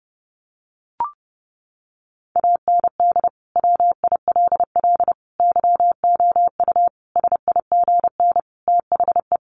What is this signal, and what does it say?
Morse code, heard as a keyed tone, about 30 wpm, 700 Hz. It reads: AND WILL YOU SIGN THE
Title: Sentences from Top 1400 Words 30wpm